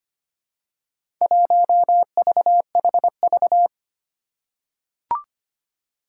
Morse code, heard as a keyed tone, about 25 wpm, 700 Hz. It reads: 1VHV
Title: Four Challenging Letter-Number Combinations 25wpm